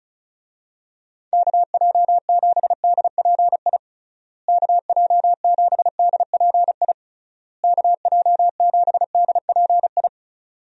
Morse code, heard as a keyed tone, about 34 wpm, 700 Hz. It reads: KJ7DPI KJ7DPI KJ7DPI
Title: USA Call Signs - 6 Characters - Repeated 3x - 35wpm